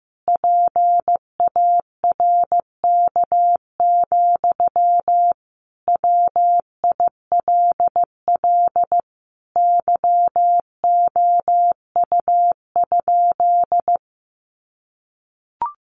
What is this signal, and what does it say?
Morse code, heard as a keyed tone, about 14 wpm, 700 Hz. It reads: PARK, WILL YOU?
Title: Sentences from Top 1400 Words 15wpm